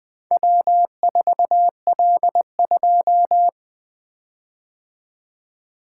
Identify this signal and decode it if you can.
Morse code, heard as a keyed tone, about 21 wpm, 700 Hz. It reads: W4L2
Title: Four Letter-Number Combinations - 20wpm